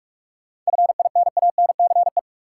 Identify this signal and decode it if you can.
Morse code, heard as a keyed tone, about 46 wpm, 700 Hz. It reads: FINANCE
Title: Sentences from Top 1500 Words - New Word Drill - 45wpm